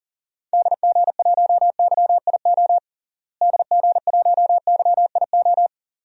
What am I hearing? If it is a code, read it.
Morse code, heard as a keyed tone, about 40 wpm, 700 Hz. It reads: DG1YIO DG1YIO
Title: DXCC-100 Call Signs - 6 Characters - Repeated 2x - 40wpm